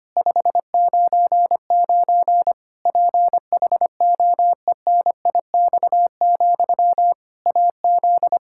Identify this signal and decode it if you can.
Morse code, heard as a keyed tone, about 25 wpm, 700 Hz. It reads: 599 PHOENIX, AZ
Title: K1USN SST QSO Practice 25wpm